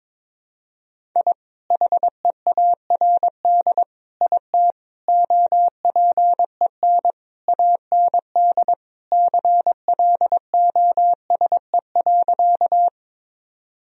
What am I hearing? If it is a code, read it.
Morse code, heard as a keyed tone, about 22 wpm, 700 Hz. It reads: I HEARD IT OPEN AND CLOSE.
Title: Sentences from Top 1000 Words - Review Part 1 - 22wpm